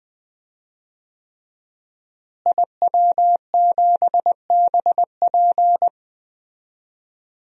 Morse code, heard as a keyed tone, about 20 wpm, 700 Hz. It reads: IW7BP
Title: DXCC-100 Call Signs - 5 Characters - Repeated 3x - 20wpm